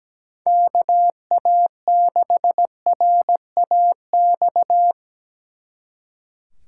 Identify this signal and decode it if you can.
Morse code, heard as a keyed tone, about 20 wpm, 700 Hz. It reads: KA6RAX
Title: Speed Racing - USA 2x3 Call Signs 17wpm